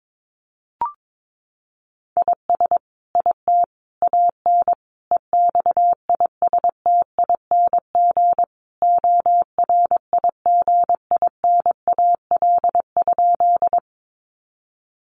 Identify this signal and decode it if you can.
Morse code, heard as a keyed tone, about 22 wpm, 700 Hz. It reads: IS IT AN EXISTING ORIGINAL?
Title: Sentences from Top 1500 Words 22wpm